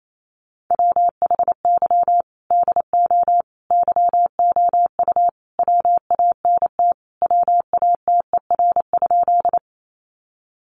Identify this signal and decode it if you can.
Morse code, heard as a keyed tone, about 28 wpm, 700 Hz. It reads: WHY DO YOU WANT WATER?
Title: Sentences from Top 200 Words 28wpm